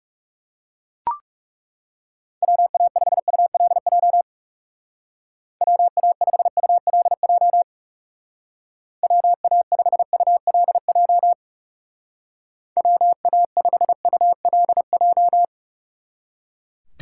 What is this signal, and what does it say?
Morse code, heard as a keyed tone, about 37 wpm, 700 Hz. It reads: WA5ULJ WA5ULJ WA5ULJ WA5ULJ
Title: Speed Racing - USA 2x3 Call Signs 30wpm